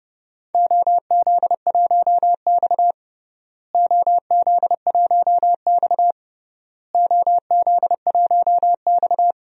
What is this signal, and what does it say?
Morse code, heard as a keyed tone, about 30 wpm, 700 Hz. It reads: OZ1X OZ1X OZ1X
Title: DXCC-100 Call Signs - 4 Characters - Repeated 3x - 30wpm